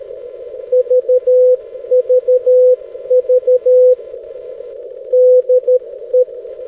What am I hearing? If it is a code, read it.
Morse code, heard as a keyed tone, about 13 wpm, 500 Hz. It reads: VVV DE
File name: 6m-lu6ee-b